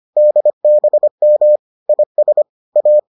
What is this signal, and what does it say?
Morse code, heard as a keyed tone, about 25 wpm, 600 Hz. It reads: DBM IS A